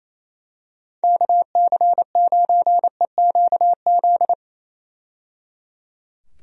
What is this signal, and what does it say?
Morse code, heard as a keyed tone, about 28 wpm, 700 Hz. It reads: KC9EQZ
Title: USA Call Signs - 6 Characters - 28wpm